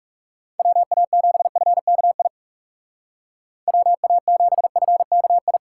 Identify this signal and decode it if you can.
Morse code, heard as a keyed tone, about 37 wpm, 700 Hz. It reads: WA7FKI WA7FKI
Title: Speed Racing - USA 2x3 Call Signs 30wpm